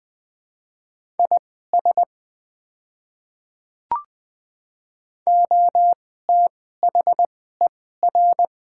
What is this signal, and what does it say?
Morse code, heard as a keyed tone, about 20 wpm, spaced as for 15 wpm, 700 Hz. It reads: IS OTHER
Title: 2x Character Spacing - Top 100 Words - 20wpm